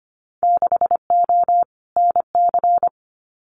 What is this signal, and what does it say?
Morse code, heard as a keyed tone, about 25 wpm, 700 Hz. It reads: 6O NC
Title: Winter Field Day Exchanges 25wpm